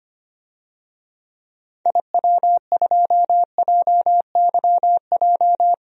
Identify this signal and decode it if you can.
Morse code, heard as a keyed tone, about 25 wpm, 690 Hz. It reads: IW2JYJ
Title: DXCC-100 Call Signs - 6 Characters - 25wpm